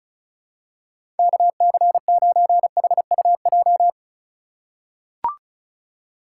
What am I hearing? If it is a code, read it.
Morse code, heard as a keyed tone, about 34 wpm, 700 Hz. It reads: KC9HUJ
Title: USA Call Signs - 6 Characters - 4x Character Spacing - 35wpm